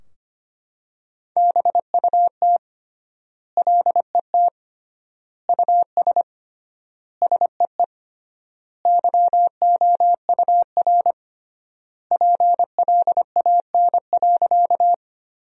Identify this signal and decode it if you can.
Morse code, heard as a keyed tone, about 25 wpm, 700 Hz. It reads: BUT LET US SEE YOUR PLAN.
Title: 3x Word-Spacing - Sentences from Top 500 Words - 25wpm